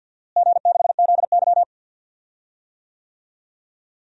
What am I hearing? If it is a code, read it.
Morse code, heard as a keyed tone, about 53 wpm, 700 Hz. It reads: G6ZY
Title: DXCC-100 Call Signs - 4 Characters - Repeated 2x - 50wpm